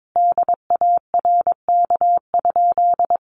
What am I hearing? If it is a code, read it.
Morse code, heard as a keyed tone, about 22 wpm, 700 Hz. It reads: DARK?
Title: Sentences from Top 500 Words 22wpm